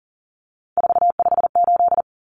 Morse code, heard as a keyed tone, about 40 wpm, 700 Hz. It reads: 458
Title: Numbers - 3-Digit 40wpm